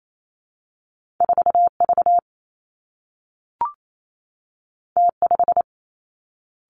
Morse code, heard as a keyed tone, about 28 wpm, 700 Hz. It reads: 4 V T 5
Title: Two Challenging Letter-Number Combinations 28wpm